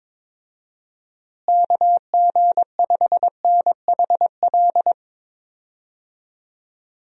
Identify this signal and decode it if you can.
Morse code, heard as a keyed tone, about 22 wpm, 700 Hz. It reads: KG5NHL
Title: USA Call Signs - 6 Characters - Repeated 2x - 22wpm